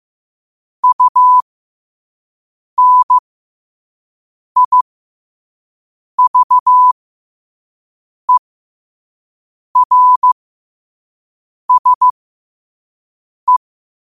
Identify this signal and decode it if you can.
Morse code, heard as a keyed tone, about 15 wpm, spaced as for 5 wpm, 1000 Hz. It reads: UNIVERSE
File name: qotdfiles / 5wpm